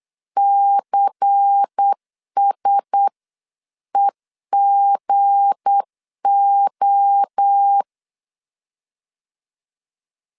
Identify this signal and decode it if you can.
Morse code, heard as a keyed tone, about 8 wpm, 800 Hz. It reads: CSEGO
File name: CW / DET / 1 9 1500C S B 8wpm 5624